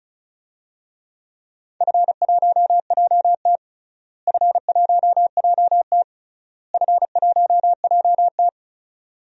Morse code, heard as a keyed tone, about 36 wpm, 700 Hz. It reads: F1JT F1JT F1JT
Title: DXCC-100 Call Signs - 4 Characters - Repeated 3x - 35wpm